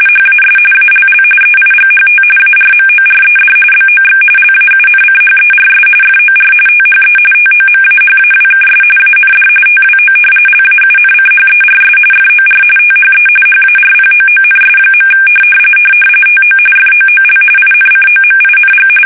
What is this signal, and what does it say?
RTTY, about 45.45 baud. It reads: HAS ALL EIGHT ARMORED BOX LAUNCHERS FOR THE TOMAHAWKS AND TWO LAUNCHERS FOR THE HARPOONS. TO FOOL INCOMING
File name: fsk850 / battleshipnj